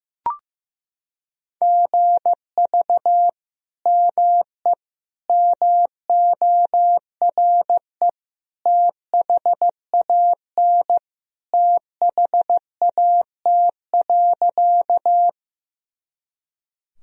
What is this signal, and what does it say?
Morse code, heard as a keyed tone, about 14 wpm, 700 Hz. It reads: GV ME MORE THAN THAT.
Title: Sentences from Top 200 Words and Abbreviations 15wpm